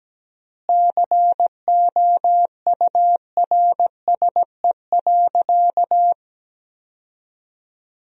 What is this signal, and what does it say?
Morse code, heard as a keyed tone, about 17 wpm, 700 Hz. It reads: COURSE.
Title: 3x Word-Spacing - Sentences from Top 500 Words - 17wpm